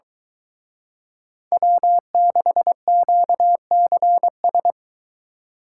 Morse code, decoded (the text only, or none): W6QCS